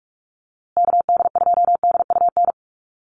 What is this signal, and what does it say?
Morse code, heard as a keyed tone, about 44 wpm, 700 Hz. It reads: KD2DUN
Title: USA Call Signs Repeated 2x 45wpm